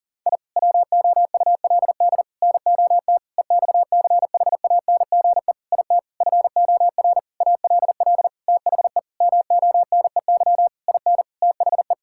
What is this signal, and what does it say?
Morse code, heard as a keyed tone, about 40 wpm, 700 Hz. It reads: I WOULD NOT EXCHANGE IT FOR ALL THE MONEY IN THE
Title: Sentences from Top 1000 Words - Review Part 6 - 40wpm